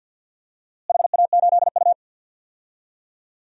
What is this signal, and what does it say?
Morse code, heard as a keyed tone, about 47 wpm, 700 Hz. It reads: HA8U
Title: DXCC-100 Call Signs - 4 Characters - Repeated 3x - 50wpm